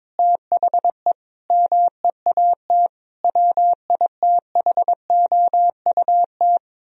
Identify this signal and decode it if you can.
Morse code, heard as a keyed tone, about 22 wpm, 700 Hz. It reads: THE MEAT WITHOUT
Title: Sentences from Top 1000 Words - Review Part 3 - 22wpm